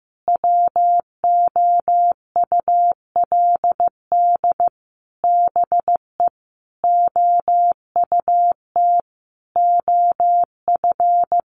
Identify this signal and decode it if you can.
Morse code, heard as a keyed tone, about 14 wpm, 700 Hz. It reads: WOULD BE OUT OF
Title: Sentences from Top 1500 Words 15wpm